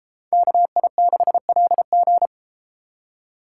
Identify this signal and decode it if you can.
Morse code, heard as a keyed tone, about 31 wpm, 700 Hz. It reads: KI6LG